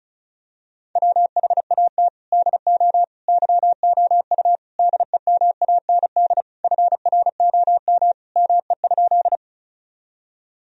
Morse code, heard as a keyed tone, about 36 wpm, 700 Hz. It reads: WHAT DO YOU DEMAND FROM ME?
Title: Sentences from Top 1000 Words - Review Part 4 - 35wpm